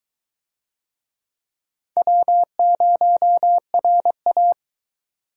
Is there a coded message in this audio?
Yes: Morse code from 2.0 to 4.5 s, heard as a keyed tone, about 22 wpm, 700 Hz. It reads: W0RA